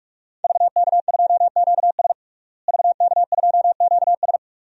Morse code, heard as a keyed tone, about 46 wpm, 700 Hz. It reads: VK2QS VK2QS
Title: DXCC-100 Call Signs - 5 Characters - Repeated 2x - 45wpm